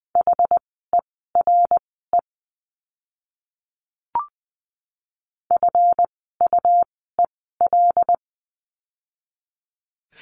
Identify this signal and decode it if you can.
Morse code, heard as a keyed tone, about 20 wpm, spaced as for 14 wpm, 700 Hz. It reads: HERE FUEL
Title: 139 Lesson 23f - Letter f - Words 20wpm